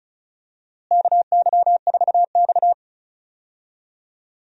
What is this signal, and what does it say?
Morse code, heard as a keyed tone, about 34 wpm, 700 Hz. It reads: KY4X
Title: USA Call Signs - 4 Characters - Repeated 3x - 35wpm